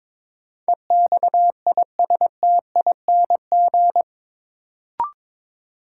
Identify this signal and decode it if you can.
Morse code, heard as a keyed tone, about 21 wpm, 700 Hz. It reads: EXISTING
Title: Sentences from Top 1500 Words - New Word Drill Alt - 22wpm